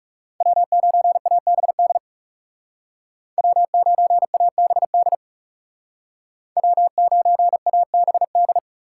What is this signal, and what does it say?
Morse code, heard as a keyed tone, about 40 wpm, 700 Hz. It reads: W9ABD W9ABD W9ABD